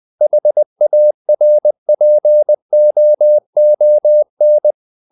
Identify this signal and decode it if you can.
Morse code, heard as a keyed tone, about 20 wpm, 600 Hz. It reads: HARPOON
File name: the old man and the sea part 2 20wpm